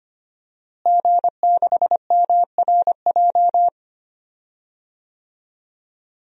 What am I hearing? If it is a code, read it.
Morse code, heard as a keyed tone, about 25 wpm, 700 Hz. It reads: G6MRJ
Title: DXCC-100 Call Signs - 5 Characters - Repeated 3x - 25wpm